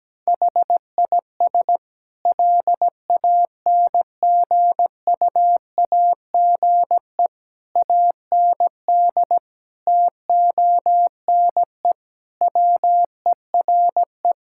Morse code, heard as a keyed tone, about 17 wpm, 700 Hz. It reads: HIS LANGUAGE AND TONE WERE
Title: Sentences from Top 1600 Words 17wpm